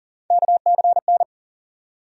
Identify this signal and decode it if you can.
Morse code, heard as a keyed tone, about 40 wpm, 700 Hz. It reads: KCN
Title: USA Call Sign Postfixes - 3 Letters 40wpm